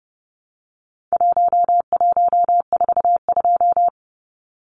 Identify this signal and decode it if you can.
Morse code, heard as a keyed tone, about 30 wpm, 700 Hz. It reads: 1142
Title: Numbers - Contest Style - 4-Digit 30wpm